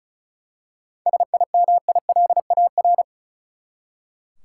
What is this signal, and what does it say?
Morse code, heard as a keyed tone, about 34 wpm, 700 Hz. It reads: SIMILAR
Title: Seven Letter Words - 35wpm